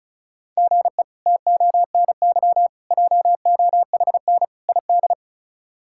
Morse code, heard as a keyed tone, about 36 wpm, 700 Hz. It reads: GE TONY JOHN ID
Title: K1USN SST QSO Practice - 35wpm